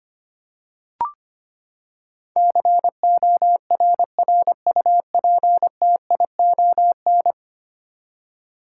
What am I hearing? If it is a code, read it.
Morse code, heard as a keyed tone, about 25 wpm, 700 Hz. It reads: CORRUPTION
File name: practice / 25wpm / Ten Letter Words - 25wpm